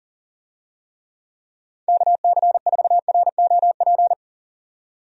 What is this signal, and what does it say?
Morse code, heard as a keyed tone, about 37 wpm, 700 Hz. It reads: KC4ROP
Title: USA Call Signs - 6 Characters - 40wpm